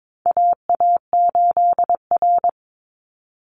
Morse code, heard as a keyed tone, about 22 wpm, 700 Hz. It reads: AA8R
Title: Call Signs Repeated 3x - 22wpm